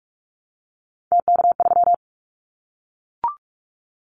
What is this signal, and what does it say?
Morse code, heard as a keyed tone, about 44 wpm, 700 Hz. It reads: TK3
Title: Callsign Prefixes for Top 100 DXCC - 45wpm